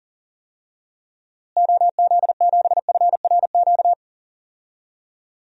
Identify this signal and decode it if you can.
Morse code, heard as a keyed tone, about 40 wpm, 700 Hz. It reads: OZ7FRQ
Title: DXCC-100 Call Signs - 6 Characters - Repeated 3x - 40wpm